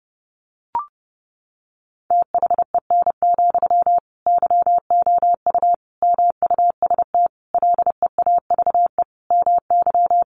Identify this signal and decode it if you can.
Morse code, heard as a keyed tone, about 30 wpm, 700 Hz. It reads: THEN, YOU MUST LEAVE MY
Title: Sentences from Top 1500 Words 30wpm